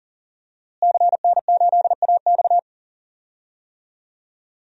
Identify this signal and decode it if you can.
Morse code, heard as a keyed tone, about 40 wpm, 700 Hz. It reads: CN8AX
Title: DXCC-100 Call Signs - 5 Characters - Repeated 2x - 40wpm